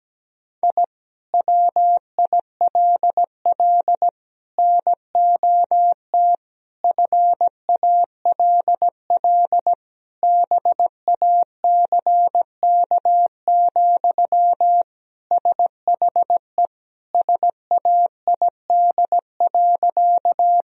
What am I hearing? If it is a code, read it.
Morse code, heard as a keyed tone, about 17 wpm, 700 Hz. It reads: I WILL NOT FALL BACK, SHE SAID.
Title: Sentences from Top 1000 Words - Review Part 2 - 17wpm